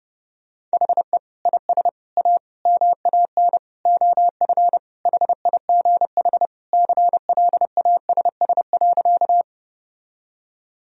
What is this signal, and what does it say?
Morse code, heard as a keyed tone, about 30 wpm, 700 Hz. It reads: HE IS A MAN OF HIGH CLASS.
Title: Sentences from Top 1000 Words - Review Part 2 - 30wpm